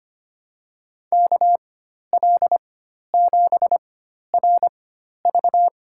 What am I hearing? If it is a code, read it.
Morse code, heard as a keyed tone, about 25 wpm, spaced as for 12 wpm, 700 Hz. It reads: KL7RV
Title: USA Call Signs - 5 Characters - 4x Character Spacing - 25wpm